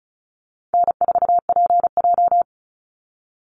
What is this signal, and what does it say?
Morse code, heard as a keyed tone, about 34 wpm, 700 Hz. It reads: N4PJ
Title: Call Signs Repeated 3x - 35wpm